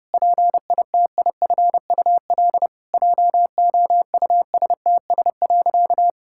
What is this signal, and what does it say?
Morse code, heard as a keyed tone, about 30 wpm, 700 Hz. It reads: PITIFUL JOUSTS.
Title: Pangram Challenge v2 - 30wpm